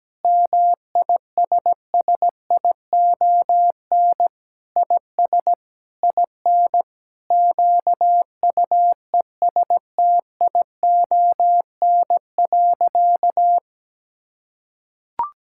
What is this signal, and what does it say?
Morse code, heard as a keyed tone, about 17 wpm, 700 Hz. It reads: MISSION IS IN QUESTION.
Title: Sentences from Top 1600 Words 17wpm